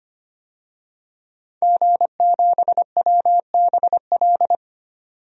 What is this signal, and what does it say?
Morse code, heard as a keyed tone, about 25 wpm, 700 Hz. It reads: G7WBL
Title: DXCC-100 Call Signs - 5 Characters - 25wpm